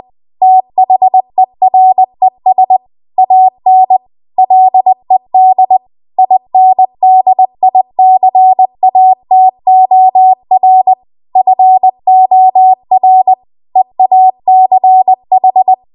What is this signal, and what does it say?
Morse code, heard as a keyed tone, about 20 wpm, 750 Hz. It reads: THERES AN LED INDICATOR FOR EACH
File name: Morse / 20WPM